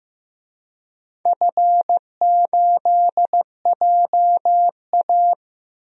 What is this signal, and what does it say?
Morse code, heard as a keyed tone, about 15 wpm, 700 Hz. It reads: F8JA
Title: DXCC-100 Call Signs - 4 Characters - 15wpm